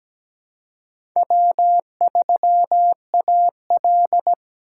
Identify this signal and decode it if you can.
Morse code, heard as a keyed tone, about 17 wpm, 700 Hz. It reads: W3AL
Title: USA Call Signs - 4 Characters - 4x Character Spacing - 17wpm